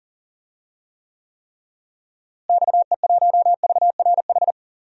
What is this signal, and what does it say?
Morse code, heard as a keyed tone, about 40 wpm, 700 Hz. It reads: XE1VRH
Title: DXCC-100 Call Signs - 6 Characters - Repeated 3x - 40wpm